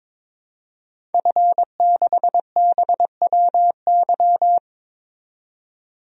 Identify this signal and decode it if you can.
Morse code, heard as a keyed tone, about 22 wpm, 700 Hz. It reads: F6BWY
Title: DXCC-100 Call Signs - 5 Characters - 22wpm